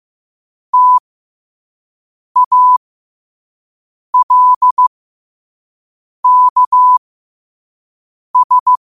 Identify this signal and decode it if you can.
Morse code, heard as a keyed tone, about 15 wpm, spaced as for 5 wpm, 1000 Hz. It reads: TALKS
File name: qotdfiles / 5wpm